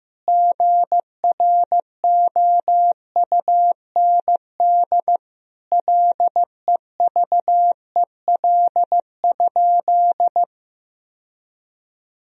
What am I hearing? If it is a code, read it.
Morse code, heard as a keyed tone, about 15 wpm, 700 Hz. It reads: GROUND LEVEL?
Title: Sentences from Top 1000 Words - Review Part 4 - 15wpm